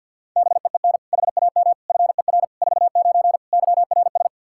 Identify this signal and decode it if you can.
Morse code, heard as a keyed tone, about 52 wpm, 700 Hz. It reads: BEEN HAM FER 49 YRS
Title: K1USN SST QSO Practice 50wpm